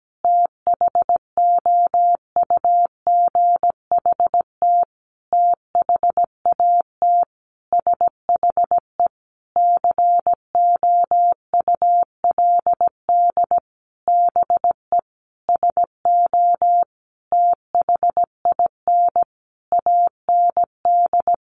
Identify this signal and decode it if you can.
Morse code, heard as a keyed tone, about 17 wpm, 700 Hz. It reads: THOUGHT THAT SHE COULD BE SO THIN AND
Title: Sentences from Top 1200 Words 17wpm